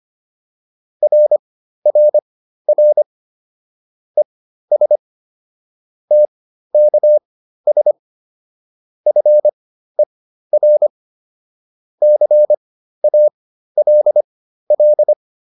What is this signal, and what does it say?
Morse code, heard as a keyed tone, about 25 wpm, spaced as for 13 wpm, 600 Hz. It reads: RRR ES TKS FER CALL
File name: QSO304 13